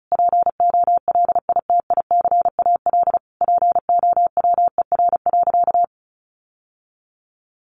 Morse code, heard as a keyed tone, about 34 wpm, 700 Hz. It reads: POLITICAL POWER.
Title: Sentences from Top 900 Words 35wpm